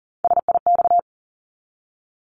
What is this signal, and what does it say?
Morse code, heard as a keyed tone, about 39 wpm, 700 Hz. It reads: SIX
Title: Top 500 Words 40wpm